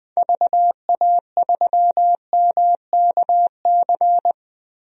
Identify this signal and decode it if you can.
Morse code, heard as a keyed tone, about 21 wpm, 700 Hz. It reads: VA3MKC
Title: DXCC-100 Call Signs - 6 Characters - Repeated 3x - 20wpm